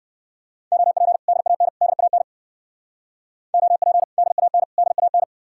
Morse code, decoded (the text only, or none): OP DAN DAN OP DAN DAN